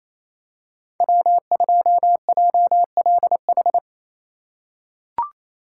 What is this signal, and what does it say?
Morse code, heard as a keyed tone, about 28 wpm, 700 Hz. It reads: W2JLH
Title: USA Call Signs - 5 Characters - 2x Character Spacing - 28wpm